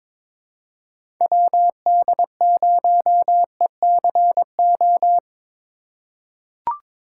Morse code, heard as a keyed tone, about 22 wpm, 700 Hz. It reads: WD0ECO
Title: K1USN SST QSO Practice - 22wpm